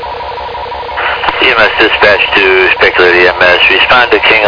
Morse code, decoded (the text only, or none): none